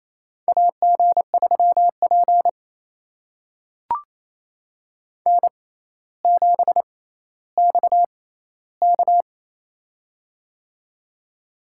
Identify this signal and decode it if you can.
Morse code, heard as a keyed tone, about 28 wpm, spaced as for 14 wpm, 700 Hz. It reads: AG3P N7XK
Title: USA Call Signs - 4 Characters - 6x Character Spacing - 28wpm